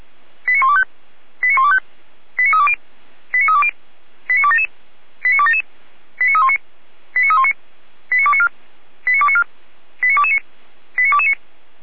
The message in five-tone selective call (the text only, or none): 89125 89125 89320 89320 89277 89277 89319 89319 89294 89294 89229 89229